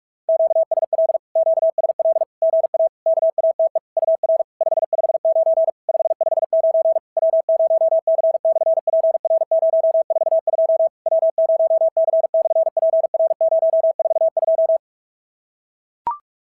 Morse code, heard as a keyed tone, about 46 wpm, 650 Hz. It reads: QSL QSL GA KATE UR 559 559 W0C/PR042 W0C/PR042
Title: SOTA QSO Practice 45wpm